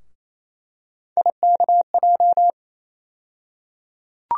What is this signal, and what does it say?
Morse code, heard as a keyed tone, about 28 wpm, 700 Hz. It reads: IKJ
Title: USA Call Sign Postfixes - 3 Letters 28wpm